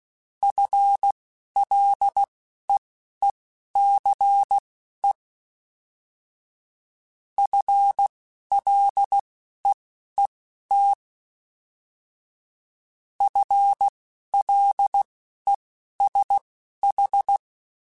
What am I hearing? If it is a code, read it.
Morse code, heard as a keyed tone, about 16 wpm, spaced as for 11 wpm, 800 Hz. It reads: FLEECE FLEET FLESH